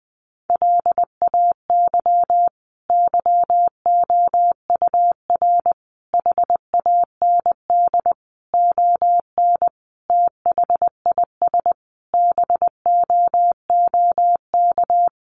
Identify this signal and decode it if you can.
Morse code, heard as a keyed tone, about 20 wpm, 700 Hz. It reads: LAY YOUR HAND ON THIS BOOK
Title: Sentences from Top 400 Words 20wpm